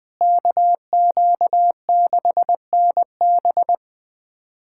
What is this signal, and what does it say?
Morse code, heard as a keyed tone, about 20 wpm, 700 Hz. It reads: KQ6NB
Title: USA Callsign DE Callsign - 5 Characters - 20wpm